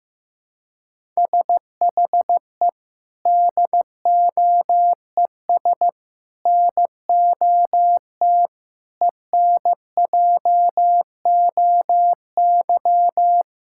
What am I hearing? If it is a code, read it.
Morse code, heard as a keyed tone, about 15 wpm, 700 Hz. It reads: SHE DOES NOT ENJOY
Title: Sentences from Top 1000 Words - Review Part 5 - 15wpm